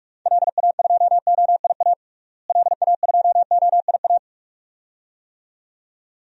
Morse code, heard as a keyed tone, about 46 wpm, 700 Hz. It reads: LA2OIA LA2OIA
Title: DXCC-100 Call Signs - 6 Characters - Repeated 2x - 45wpm